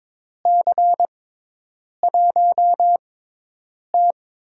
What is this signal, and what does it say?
Morse code, heard as a keyed tone, about 22 wpm, spaced as for 8 wpm, 700 Hz. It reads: C1T
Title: USA Call Signs - 4 Characters - 6x Character Spacing - 22wpm